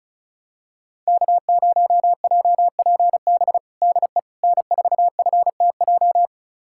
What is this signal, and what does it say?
Morse code, heard as a keyed tone, about 36 wpm, 700 Hz. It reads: K0JPB DE N4FTJ
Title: USA Callsign DE Callsign - 5 Characters - 35wpm